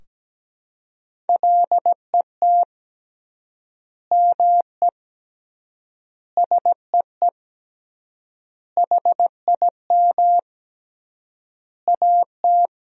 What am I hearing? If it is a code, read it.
Morse code, heard as a keyed tone, about 17 wpm, 700 Hz. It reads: LET ME SEE HIM AT
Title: 3x Word-Spacing - Sentences from Top 300 Words - 17wpm